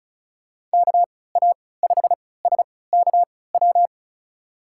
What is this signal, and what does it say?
Morse code, heard as a keyed tone, about 34 wpm, spaced as for 20 wpm, 700 Hz. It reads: KA5SKW
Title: USA Call Signs - 6 Characters - 3x Character Spacing - 35wpm